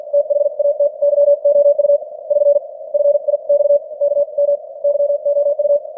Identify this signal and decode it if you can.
Morse code, heard as a keyed tone, about 50 wpm, 590 Hz. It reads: THAT YOU <SN> FI X K M YOU
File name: archived / CWBros